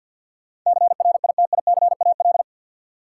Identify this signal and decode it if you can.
Morse code, heard as a keyed tone, about 50 wpm, 700 Hz. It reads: CRITICAL
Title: Sentences from Top 1100 Words - New Word Drill - 50wpm